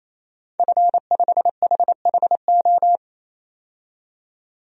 Morse code, heard as a keyed tone, about 28 wpm, 700 Hz. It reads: F5HHO
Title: DXCC-100 Call Signs - 5 Characters - 28wpm